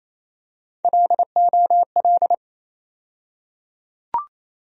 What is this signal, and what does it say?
Morse code, heard as a keyed tone, about 28 wpm, 700 Hz. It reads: LOL
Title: USA Call Sign Postfixes - 3 Letters 28wpm